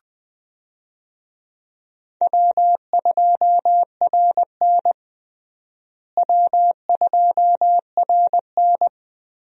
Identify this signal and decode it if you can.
Morse code, heard as a keyed tone, about 20 wpm, 700 Hz. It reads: W2RN W2RN
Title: USA Call Signs - 4 Characters - Repeated 3x - 20wpm